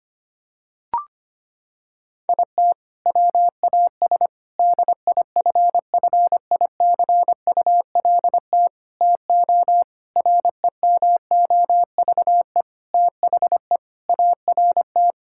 Morse code, heard as a keyed tone, about 25 wpm, 700 Hz. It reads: IT WAS DIFFICULT TO REMOVE THE ART
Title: Sentences from Top 1000 Words - Review Part 6 - 25wpm